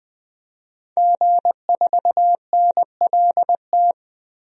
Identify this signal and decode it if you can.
Morse code, heard as a keyed tone, about 20 wpm, 700 Hz. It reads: G4NLT
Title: DXCC-100 Call Signs - 5 Characters - 20wpm